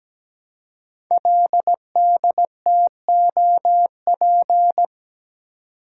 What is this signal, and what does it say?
Morse code, heard as a keyed tone, about 17 wpm, 700 Hz. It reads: LDTOP
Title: Code Groups - 5 Letters - 17wpm